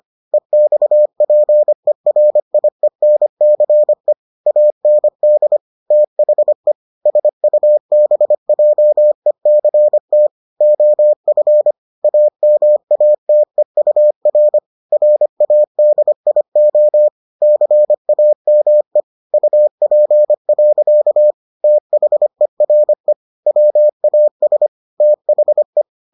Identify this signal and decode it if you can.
Morse code, heard as a keyed tone, about 25 wpm, 600 Hz. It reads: EXPERIENCE AND THE SUBJECT OF AMATEUR RADIO CAME UP. THERE WAS THE